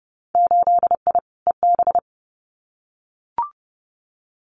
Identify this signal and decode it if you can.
Morse code, heard as a keyed tone, about 30 wpm, 700 Hz. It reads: 8I EB
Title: Winter Field Day Exchanges Encore 30wpm